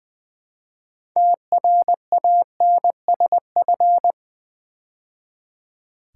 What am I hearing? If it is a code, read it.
Morse code, heard as a keyed tone, about 20 wpm, 700 Hz. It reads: TRANSF
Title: Build A Word - 3 Letters At A Time - 8-Letter Words - 20wpm